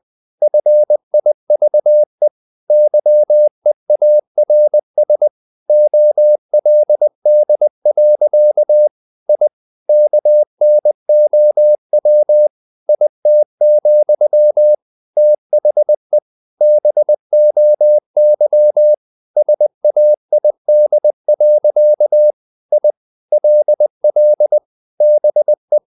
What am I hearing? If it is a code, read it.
Morse code, heard as a keyed tone, about 20 wpm, 600 Hz. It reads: FIVE YEARS OLD. I KNOW IT, THE BOY SAID. I LL BE
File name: the old man and the sea / the old man and the sea part 1 20wpm